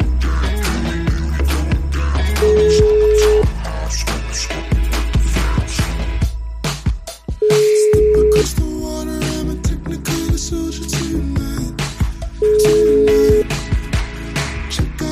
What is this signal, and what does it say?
Ringback tone (Europe, 425 Hz).